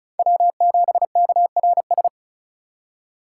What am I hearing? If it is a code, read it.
Morse code, heard as a keyed tone, about 36 wpm, 700 Hz. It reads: W7KRS
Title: K1USN SST QSO Practice - 35wpm